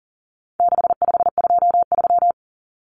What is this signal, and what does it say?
Morse code, heard as a keyed tone, about 40 wpm, 700 Hz. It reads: 6523
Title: Numbers - 4-Digit 40wpm